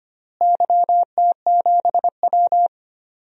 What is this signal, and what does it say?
Morse code, heard as a keyed tone, about 25 wpm, 700 Hz. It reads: YT7W
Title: DXCC-100 Call Signs - 4 Characters - Repeated 3x - 25wpm